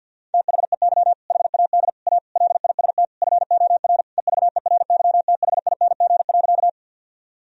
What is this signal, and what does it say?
Morse code, heard as a keyed tone, about 50 wpm, 700 Hz. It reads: THEY HAD A LIST FOR EVERYTHING.
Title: Sentences from Top 1000 Words - Review Part 2 - 50wpm